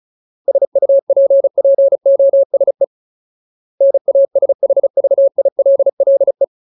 Morse code, heard as a keyed tone, about 34 wpm, 550 Hz. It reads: SUPPOSE NASHVILLE
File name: Ultimate Word List / 35WPM Ultimate